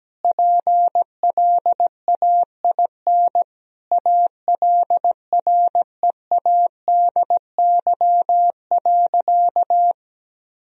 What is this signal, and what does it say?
Morse code, heard as a keyed tone, about 17 wpm, 700 Hz. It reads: PLAIN ALREADY.
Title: Sentences from Top 1000 Words - Review Part 2 - 17wpm